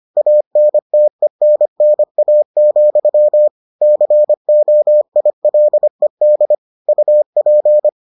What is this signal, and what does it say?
Morse code, heard as a keyed tone, about 25 wpm, 600 Hz. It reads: ANTENNA, COILED UP